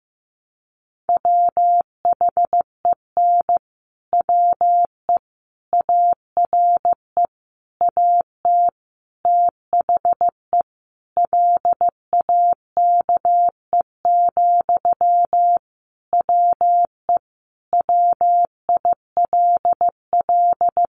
Morse code, heard as a keyed tone, about 15 wpm, 700 Hz. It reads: WHEN WE ARE AT THE LAKE, WE WILL
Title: Sentences from Top 1100 Words 15wpm